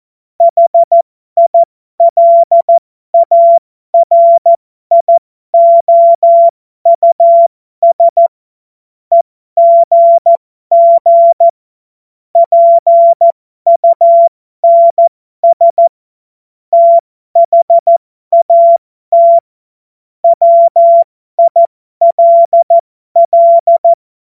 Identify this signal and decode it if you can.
Morse code, heard as a keyed tone, about 14 wpm, spaced as for 12 wpm, 680 Hz. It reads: HILARIOUS EGG PUNS THAT WILL
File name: Morsekurs / Witze